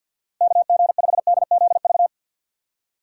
Morse code, heard as a keyed tone, about 50 wpm, 700 Hz. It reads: KG5DZV
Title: USA Call Signs - 6 Characters - 6x Character Spacing - 50wpm